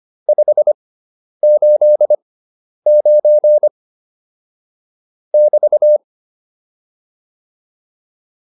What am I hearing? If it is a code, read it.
Morse code, heard as a keyed tone, about 25 wpm, spaced as for 10 wpm, 600 Hz. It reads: 589 <BT>